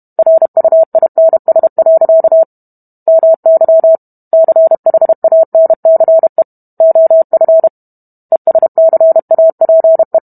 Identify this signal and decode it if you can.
Morse code, heard as a keyed tone, about 32 wpm, 650 Hz. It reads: RUINS. MY CHANCE OF ESCAPE